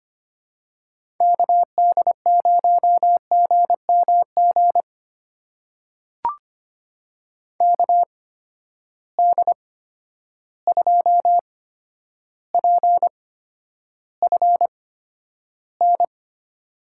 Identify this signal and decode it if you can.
Morse code, heard as a keyed tone, about 25 wpm, spaced as for 11 wpm, 700 Hz. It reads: KD0GMG KD2PFN
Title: USA Call Signs - 6 Characters - 8x Character Spacing - 25wpm